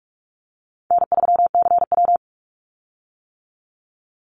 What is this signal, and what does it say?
Morse code, heard as a keyed tone, about 46 wpm, 700 Hz. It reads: N3CW
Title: Easy Call Signs - 45wpm